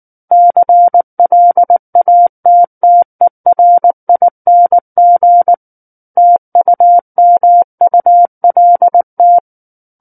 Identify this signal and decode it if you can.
Morse code, heard as a keyed tone, about 19 wpm, 700 Hz. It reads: CLATTERING TUMULT
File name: Book One / 10-Chapter 10 - 19 WPM